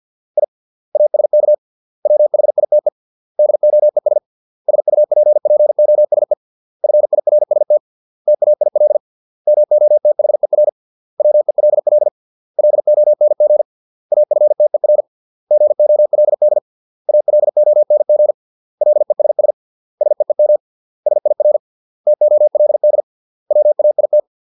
Morse code, heard as a keyed tone, about 50 wpm, 600 Hz. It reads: I ASK WHITE DOES SUPPOSE FIRST TAIL MOTHER WELL LONG AFTER GOLD ALONG LESS SEEM SIR TOLD WAIT